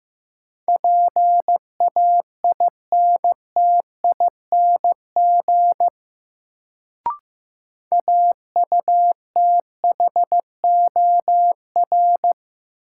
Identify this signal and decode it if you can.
Morse code, heard as a keyed tone, about 15 wpm, 700 Hz. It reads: PAINTING AUTHOR
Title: Sentences from Top 1200 Words - New Word Drill Alt - 15wpm